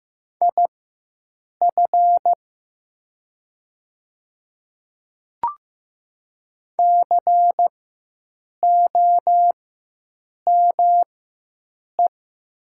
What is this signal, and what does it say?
Morse code, heard as a keyed tone, about 15 wpm, spaced as for 7 wpm, 700 Hz. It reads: IF COME
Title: 4x Character Spacing - Top 100 Words - 15wpm